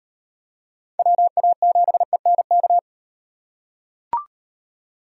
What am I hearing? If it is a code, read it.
Morse code, heard as a keyed tone, about 38 wpm, 700 Hz. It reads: WA7ENK